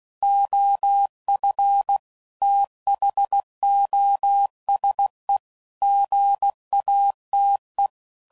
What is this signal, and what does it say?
Morse code, heard as a keyed tone, about 16 wpm, 800 Hz. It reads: OF THOSE GATE
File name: CW Code Groups @15WPM / Technical Subjects / AND Gate